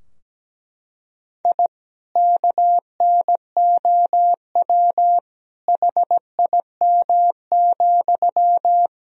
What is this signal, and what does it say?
Morse code, heard as a keyed tone, about 17 wpm, 700 Hz. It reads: I KNOW HIM,
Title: Sentences from Top 1000 Words - Review Part 4 - 17wpm